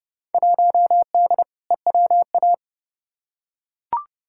Morse code, heard as a keyed tone, about 32 wpm, 700 Hz. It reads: 1D EWA
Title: Field Day Exchanges - 30wpm